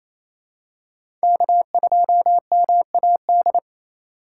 Morse code, heard as a keyed tone, about 28 wpm, 700 Hz. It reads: K2MAD
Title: USA Call Signs - 3x Character Spacing - 28wpm